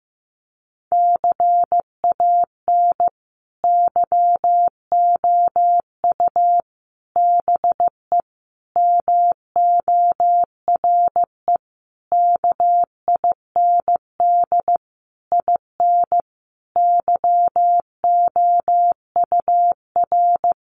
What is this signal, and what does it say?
Morse code, heard as a keyed tone, about 15 wpm, 700 Hz. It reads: CAN YOU BE MORE KIND IN YOUR
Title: Sentences from Top 1100 Words 15wpm